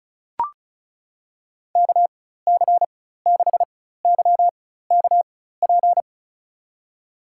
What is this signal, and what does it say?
Morse code, heard as a keyed tone, about 34 wpm, spaced as for 16 wpm, 700 Hz. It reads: KC6YKP
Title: USA Call Signs - 4x Character Spacing - 35wpm